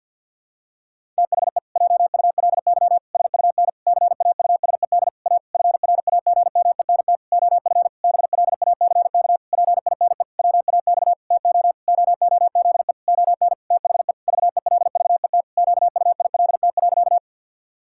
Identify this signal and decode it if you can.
Morse code, heard as a keyed tone, about 52 wpm, 700 Hz. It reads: THE JULY SUN CAUSED A FRAGMENT OF BLACK PINE WAX TO OOZE ON THE VELVET QUILT.
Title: Pangram Challenge v2 - 50wpm